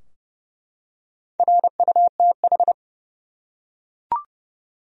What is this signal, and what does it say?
Morse code, heard as a keyed tone, about 28 wpm, 700 Hz. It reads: RUTH
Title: USA Common Names - Max 4 Letters - 30wpm